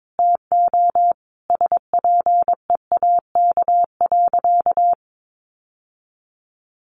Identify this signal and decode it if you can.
Morse code, heard as a keyed tone, about 22 wpm, 700 Hz. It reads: TO SPEAK.
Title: Sentences from Top 1500 Words 22wpm